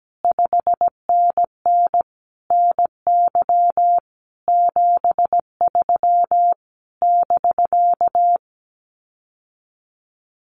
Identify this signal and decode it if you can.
Morse code, heard as a keyed tone, about 17 wpm, 700 Hz. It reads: 5NN NY 73 <BK>
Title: Common QSO Phrases Encore 17wpm